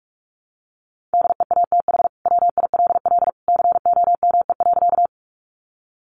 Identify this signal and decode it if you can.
Morse code, heard as a keyed tone, about 44 wpm, 700 Hz. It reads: DEATH WILL COME.
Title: Sentences from Top 1100 Words 45wpm